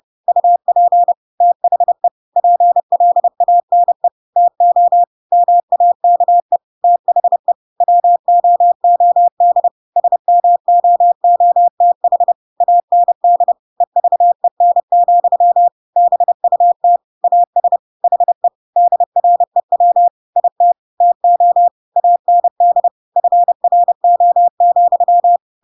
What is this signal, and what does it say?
Morse code, heard as a keyed tone, about 30 wpm, 700 Hz. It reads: UP THE PLANE TO MAKE THE WOOD SMOOTH AND EVEN, BUT AS HE DREW IT TO AND FRO,
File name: Pinocchio / Pinocchio ch1 30 30 ik1hge